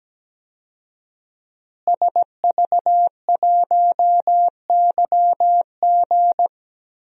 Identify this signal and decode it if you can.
Morse code, heard as a keyed tone, about 17 wpm, 700 Hz. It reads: SV1YG
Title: DXCC-100 Call Signs - 5 Characters - Repeated 2x - 17wpm